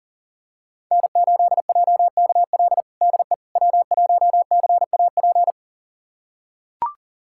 Morse code, heard as a keyed tone, about 37 wpm, 700 Hz. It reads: N8JKL DE W1CAP
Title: USA Callsign DE Callsign - 5 Characters - 40wpm